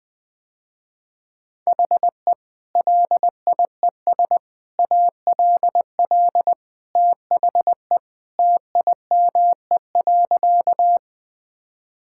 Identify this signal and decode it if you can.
Morse code, heard as a keyed tone, about 21 wpm, 700 Hz. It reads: HE LIES ALL THE TIME.
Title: Sentences with 3-Word Phrases - Spoken Beginning - 20wpm